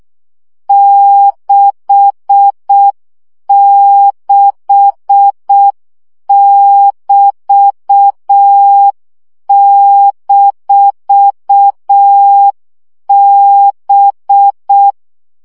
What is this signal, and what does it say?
Morse code, heard as a keyed tone, about 6 wpm, 800 Hz. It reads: 66<BT>-B
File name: CW gyakorlas b 6 egyenlo minusz 5 baud